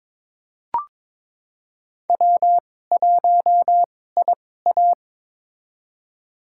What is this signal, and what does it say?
Morse code, heard as a keyed tone, about 22 wpm, spaced as for 16 wpm, 700 Hz. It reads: W1IA
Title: USA Call Signs - 4 Characters - 2x Character Spacing - 22wpm